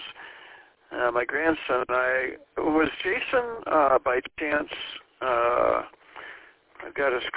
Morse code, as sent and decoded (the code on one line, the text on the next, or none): none